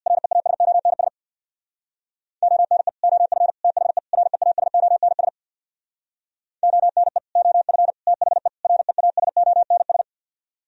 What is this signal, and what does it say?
Morse code, heard as a keyed tone, about 64 wpm, 700 Hz. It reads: REASONS ONE OF THE REASONS ONE OF THE REASONS